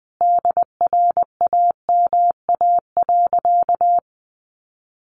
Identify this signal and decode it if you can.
Morse code, heard as a keyed tone, about 19 wpm, 700 Hz. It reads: DRAMA.
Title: Sentences from Top 1500 Words 20wpm